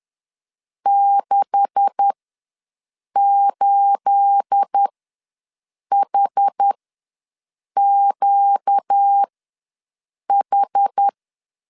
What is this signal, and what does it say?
Morse code, heard as a keyed tone, about 10 wpm, spaced as for 6 wpm, 800 Hz. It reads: 68HQH
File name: LEZIONI / 10 SIGNS / 10A SLASH ws 10wpm 4121